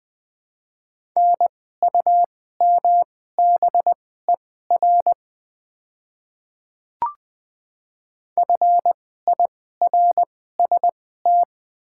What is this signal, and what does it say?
Morse code, heard as a keyed tone, about 20 wpm, spaced as for 15 wpm, 700 Hz. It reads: NUMBER FIRST
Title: 2x Character Spacing - Top 300 Words - 20wpm